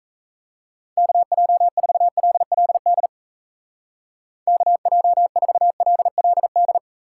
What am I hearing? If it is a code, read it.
Morse code, heard as a keyed tone, about 37 wpm, 700 Hz. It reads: KJ4LLD KJ4LLD